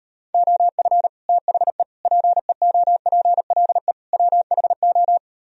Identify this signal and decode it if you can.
Morse code, heard as a keyed tone, about 38 wpm, 700 Hz. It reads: OF THE PEOPLE WHO